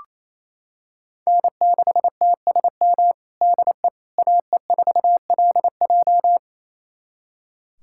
Morse code, heard as a keyed tone, about 27 wpm, 700 Hz. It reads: N6TSM DE AE4LJ
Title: USA Callsign DE Callsign - 5 Characters - 28wpm